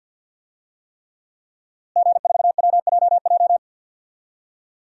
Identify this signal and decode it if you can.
Morse code, heard as a keyed tone, about 51 wpm, 700 Hz. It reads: G4WJJ
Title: DXCC-100 Call Signs - 5 Characters - Repeated 3x - 50wpm